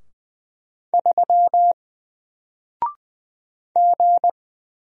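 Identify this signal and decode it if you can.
Morse code, heard as a keyed tone, about 20 wpm, 700 Hz. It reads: 3 G
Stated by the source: Single letter and number Morse drill at 20 wpm, from N4LCD.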